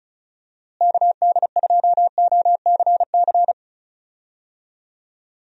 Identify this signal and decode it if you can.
Morse code, heard as a keyed tone, about 34 wpm, 700 Hz. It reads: KD2OCC
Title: USA Call Signs - 6 Characters - Repeated 2x - 35wpm